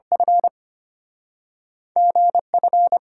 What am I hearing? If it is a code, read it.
Morse code, heard as a keyed tone, about 34 wpm, 700 Hz. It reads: F GF